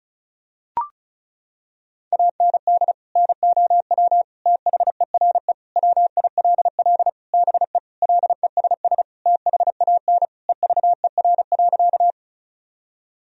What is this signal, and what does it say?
Morse code, heard as a keyed tone, about 36 wpm, 700 Hz. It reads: AND NOW THERE WILL BE LESS THAN EVER.
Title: Sentences from Top 1000 Words - Review Part 2 - 35wpm